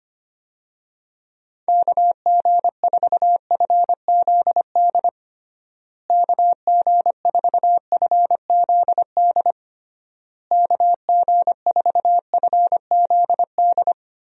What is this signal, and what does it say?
Morse code, heard as a keyed tone, about 25 wpm, 700 Hz. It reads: KG4FZD KG4FZD KG4FZD
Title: USA Call Signs - 6 Characters - Repeated 3x - 25wpm